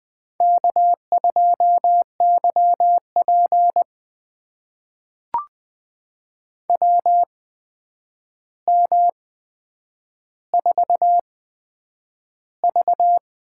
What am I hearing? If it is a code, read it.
Morse code, heard as a keyed tone, about 20 wpm, spaced as for 9 wpm, 700 Hz. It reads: K2YP WM4V
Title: USA Call Signs - 4 Characters - 8x Character Spacing - 20wpm